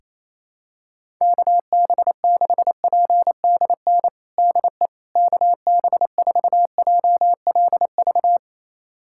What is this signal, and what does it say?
Morse code, heard as a keyed tone, about 27 wpm, 700 Hz. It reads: KB6PDN DE KB4JLV
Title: USA Callsign DE Callsign - 6 Characters - 28wpm